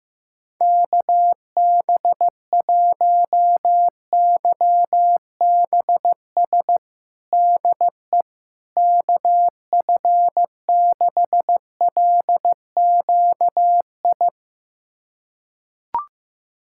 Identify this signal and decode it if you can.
Morse code, heard as a keyed tone, about 15 wpm, 700 Hz. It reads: KB1YBS DE KF6LQI
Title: USA Callsign DE Callsign - 6 Characters - 15wpm